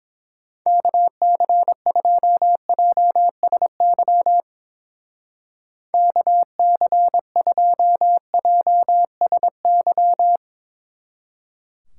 Morse code, heard as a keyed tone, about 27 wpm, 700 Hz. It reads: KC2JSY KC2JSY
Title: Speed Racing - USA Call Signs 22wpm